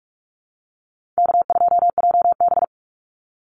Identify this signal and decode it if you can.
Morse code, heard as a keyed tone, about 44 wpm, 700 Hz. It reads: K2JB
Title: NA SOTA Chaser Callsigns 45wpm